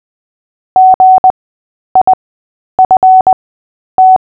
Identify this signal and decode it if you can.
Morse code, heard as a keyed tone, about 20 wpm, spaced as for 10 wpm, 750 Hz. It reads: GIFT
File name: Short-Story-7-10wpm